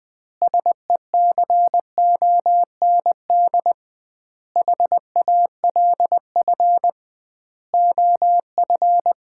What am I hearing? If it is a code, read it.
Morse code, heard as a keyed tone, about 20 wpm, 700 Hz. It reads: SECOND HALF OF
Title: 2x Word-Spacing - Sets of 5 Words - 20wpm